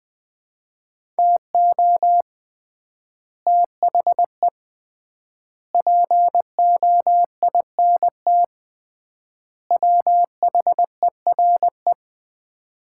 Morse code, heard as a keyed tone, about 21 wpm, 700 Hz. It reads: TO THE POINT WHERE
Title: 3x Word-Spacing - Sets of 4 Words - 20wpm